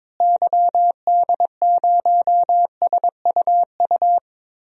Morse code, heard as a keyed tone, about 22 wpm, 700 Hz. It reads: YD0SUU
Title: DXCC-100 Call Signs - 6 Characters - Repeated 2x - 22wpm